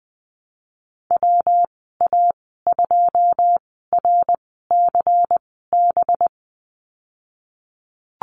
Morse code, heard as a keyed tone, about 20 wpm, spaced as for 15 wpm, 700 Hz. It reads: WA2RCB
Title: 193 Lesson 31g - Letter v - Call Signs 20wpm